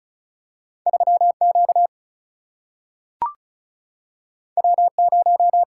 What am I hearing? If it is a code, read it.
Morse code, heard as a keyed tone, about 34 wpm, 700 Hz. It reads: 3Q W0
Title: Single-Letter Single-Number 35wpm